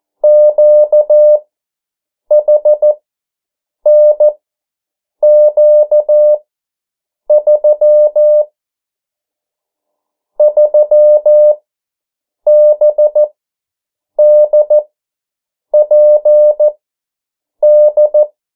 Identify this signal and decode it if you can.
Morse code, heard as a keyed tone, about 14 wpm, spaced as for 7 wpm, 600 Hz. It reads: QHNQ3 3BDPD
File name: morse mp3 / 09 urok 8